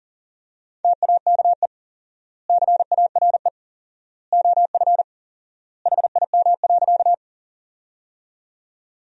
Morse code, heard as a keyed tone, about 40 wpm, 700 Hz. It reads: TAKE CARE OF HIM.
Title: 4x Word-Spacing - Sentences from Top 300 Words - 40wpm